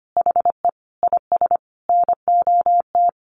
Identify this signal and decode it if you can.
Morse code, heard as a keyed tone, about 25 wpm, 700 Hz. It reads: HE IS NOT
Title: Sentences from Top 900 Words 25wpm